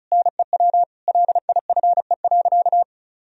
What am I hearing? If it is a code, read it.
Morse code, heard as a keyed tone, about 34 wpm, 700 Hz. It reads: NEW LIFE.
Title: Sentences from Top 1000 Words - Review Part 1 - 35wpm